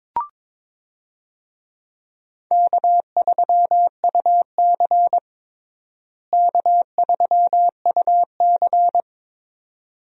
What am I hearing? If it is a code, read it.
Morse code, heard as a keyed tone, about 22 wpm, 700 Hz. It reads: K3UC K3UC
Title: USA Call Signs - 4 Characters - Repeated 2x - 22wpm